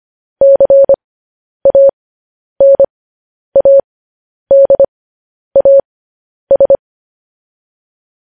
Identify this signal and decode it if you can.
Morse code, heard as a keyed tone, about 25 wpm, spaced as for 10 wpm, 550 Hz. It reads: CANADAS
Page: news headlines in morse code at 10 wpm